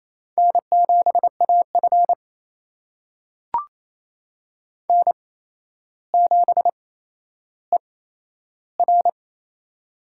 Morse code, heard as a keyed tone, about 28 wpm, spaced as for 12 wpm, 700 Hz. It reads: N7AF N7ER
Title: USA Call Signs - 4 Characters - 8x Character Spacing - 28wpm